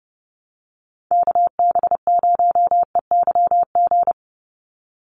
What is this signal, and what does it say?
Morse code, heard as a keyed tone, about 30 wpm, 700 Hz. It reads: KB0EYG
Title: USA Call Signs 30wpm